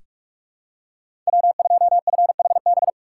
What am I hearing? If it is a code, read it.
Morse code, heard as a keyed tone, about 46 wpm, 700 Hz. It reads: W2FHB
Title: USA Call Signs - 3x Character Spacing - 45wpm